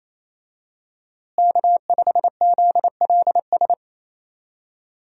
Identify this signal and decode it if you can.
Morse code, heard as a keyed tone, about 28 wpm, 700 Hz. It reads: K5ZLS
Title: USA Call Signs - 5 Characters - Repeated 3x - 28wpm